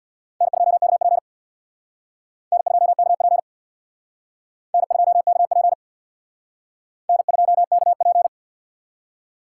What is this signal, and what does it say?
Morse code, heard as a keyed tone, about 62 wpm, 700 Hz. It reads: N2KP N2KP N2KP N2KP
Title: Speed Racing - USA 1x2 Call Signs 50wpm